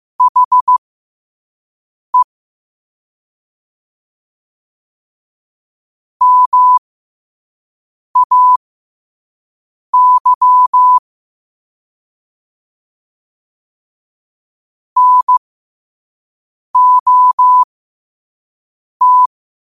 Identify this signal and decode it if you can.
Morse code, heard as a keyed tone, about 15 wpm, spaced as for 5 wpm, 1000 Hz. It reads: HE MAY NOT
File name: qotdfiles / 5wpm